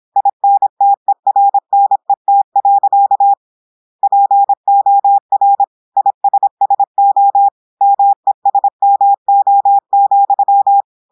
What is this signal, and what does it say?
Morse code, heard as a keyed tone, about 26 wpm, 800 Hz. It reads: INTERNET. POR ISSO MESMO,
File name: portugais texte en clair 2 vitesse 26 mots minutes